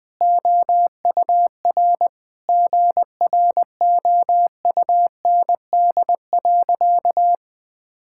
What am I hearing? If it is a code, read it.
Morse code, heard as a keyed tone, about 20 wpm, 700 Hz. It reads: OUR GROUND.
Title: Sentences from Top 1000 Words - Review Part 5 - 20wpm